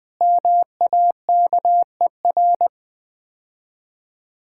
Morse code, heard as a keyed tone, about 21 wpm, 700 Hz. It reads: MAKER
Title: Easy Words - Speed-Racing with Hints - 20wpm